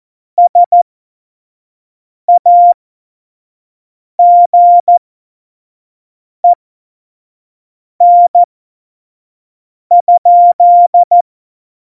Morse code, heard as a keyed tone, about 14 wpm, spaced as for 5 wpm, 700 Hz. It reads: SAGEN?
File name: Morsekurs / Witze / Witz 07. Januar 2020, 25 F70